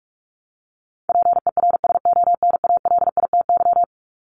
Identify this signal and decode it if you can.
Morse code, heard as a keyed tone, about 44 wpm, 700 Hz. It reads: PERSONALITY
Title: Top 1251-1500 Words 45wpm